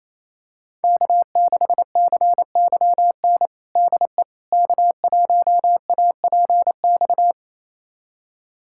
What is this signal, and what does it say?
Morse code, heard as a keyed tone, about 28 wpm, 690 Hz. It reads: K6CYN DE K1APX
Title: USA Callsign DE Callsign - 5 Characters - 28wpm